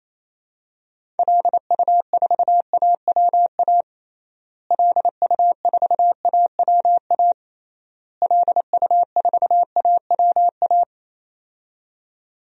Morse code, heard as a keyed tone, about 28 wpm, 700 Hz. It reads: LU4AWA LU4AWA LU4AWA
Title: DXCC-100 Call Signs - 6 Characters - Repeated 3x - 28wpm